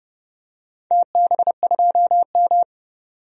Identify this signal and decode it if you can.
Morse code, heard as a keyed tone, about 32 wpm, 700 Hz. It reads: TB2M
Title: DXCC-100 Call Signs - 4 Characters - 30wpm